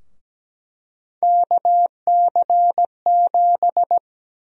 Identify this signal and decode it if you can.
Morse code, heard as a keyed tone, about 17 wpm, 700 Hz. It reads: KC7
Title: USA Call Sign Prefixes - Repeated 2x - 17wpm